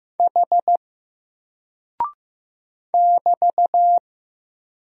Morse code, heard as a keyed tone, about 15 wpm, 700 Hz. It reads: H <BT>
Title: Single Character (Letter:Number:Punctuation:Prosign) Alt 15wpm